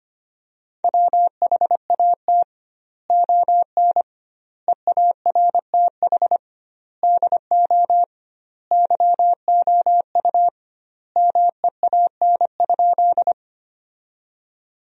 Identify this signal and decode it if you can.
Morse code, heard as a keyed tone, about 25 wpm, 700 Hz. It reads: WHAT ON EARTH DO YOU MEAN?
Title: 2x Word-Spacing - Sentences from Top 200 Words - 25wpm